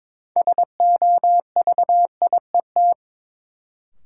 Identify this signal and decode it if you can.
Morse code, heard as a keyed tone, about 22 wpm, 700 Hz. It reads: SOVIET
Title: Sets of 5 Words Encore - New Word Drill - 22wpm